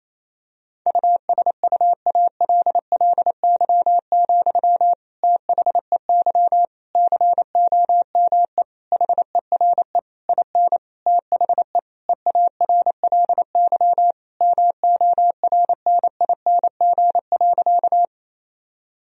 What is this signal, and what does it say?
Morse code, heard as a keyed tone, about 27 wpm, 700 Hz. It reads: USUALLY, THEY COME HERE IN THE EARLY MORNING.
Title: Sentences from Top 1000 Words - Review Part 3 - 28wpm